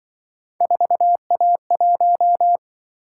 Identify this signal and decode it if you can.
Morse code, heard as a keyed tone, about 24 wpm, 700 Hz. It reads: 4A1
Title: Speed-Racing - Three Letter-Number Combinations - 20wpm